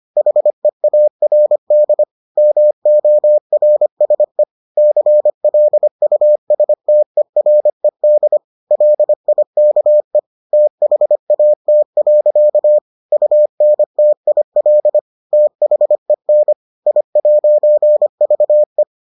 Morse code, heard as a keyed tone, about 25 wpm, 600 Hz. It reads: HEARD MORSE CLUSTERED LIKE THAT. UNTIL THEN I'VE